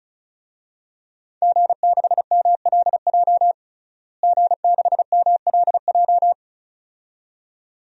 Morse code, heard as a keyed tone, about 34 wpm, 700 Hz. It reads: G6MLJ G6MLJ
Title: DXCC-100 Call Signs - 5 Characters - Repeated 2x - 35wpm